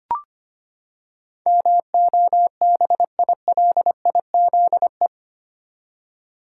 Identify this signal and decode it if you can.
Morse code, heard as a keyed tone, about 25 wpm, 700 Hz. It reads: MOBILIZE
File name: Eight Letter Words - 25wpm